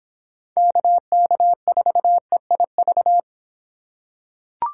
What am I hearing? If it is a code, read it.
Morse code, heard as a keyed tone, about 25 wpm, 700 Hz. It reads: KK4EIV